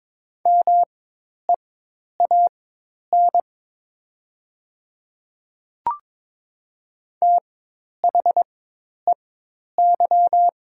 Morse code, heard as a keyed tone, about 22 wpm, spaced as for 10 wpm, 700 Hz. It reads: MEAN THEY
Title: 4x Character Spacing - Top 200 Words - 22wpm